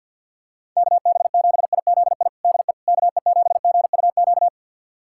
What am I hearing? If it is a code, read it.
Morse code, heard as a keyed tone, about 53 wpm, 700 Hz. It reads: KB7IZI DE KE7GUQ
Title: USA Callsign DE Callsign - 6 Characters - 50wpm